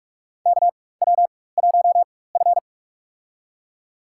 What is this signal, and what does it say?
Morse code, heard as a keyed tone, about 46 wpm, spaced as for 21 wpm, 690 Hz. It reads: KW1F
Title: USA Call Signs - 4 Characters - 4x Character Spacing - 45wpm